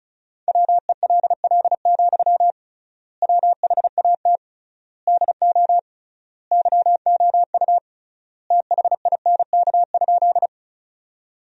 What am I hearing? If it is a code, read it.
Morse code, heard as a keyed tone, about 36 wpm, 700 Hz. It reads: WELL, WHAT DO YOU THINK?
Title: 3x Word-Spacing - Sentences from Top 200 Words - 35wpm